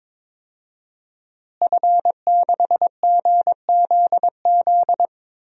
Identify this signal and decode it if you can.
Morse code, heard as a keyed tone, about 22 wpm, 700 Hz. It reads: F6GZZ
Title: DXCC-100 Call Signs - 5 Characters - Repeated 3x - 22wpm